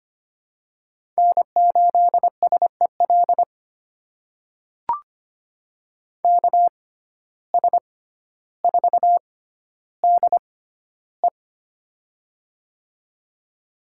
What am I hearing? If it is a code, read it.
Morse code, heard as a keyed tone, about 25 wpm, spaced as for 13 wpm, 700 Hz. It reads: N8SEL KS4DE
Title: USA Call Signs - 5 Characters - 6x Character Spacing - 25wpm